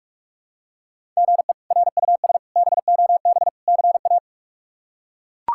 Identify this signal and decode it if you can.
Morse code, heard as a keyed tone, about 42 wpm, 700 Hz. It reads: GE RUS BOB CA
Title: K1USN SST QSO Practice - 45wpm